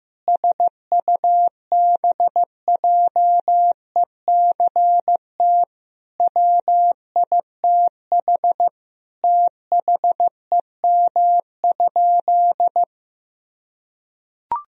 Morse code, heard as a keyed tone, about 15 wpm, 700 Hz. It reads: SUBJECT WITH THEM?
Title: Sentences from Top 1000 Words - Review Part 1 - 15wpm